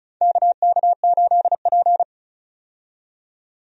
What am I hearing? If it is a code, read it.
Morse code, heard as a keyed tone, about 36 wpm, 700 Hz. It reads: KK8P
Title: USA Call Signs - 4 Characters - 35wpm